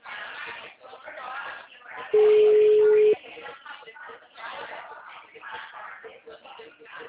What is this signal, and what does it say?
Ringback tone (Europe, 425 Hz), with speech.
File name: out- -2010- - - .6363